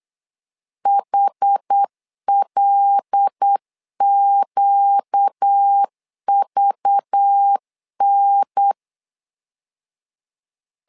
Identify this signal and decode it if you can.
Morse code, heard as a keyed tone, about 8 wpm, 800 Hz. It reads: HLQVN
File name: LEZIONI / 5 LBVC / 5 LBV 8wpm 2837